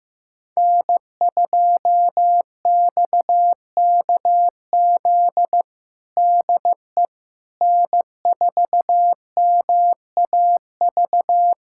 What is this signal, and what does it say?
Morse code, heard as a keyed tone, about 15 wpm, 700 Hz. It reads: N2XKZ DE N4MAV
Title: USA Callsign DE Callsign - 5 Characters - 15wpm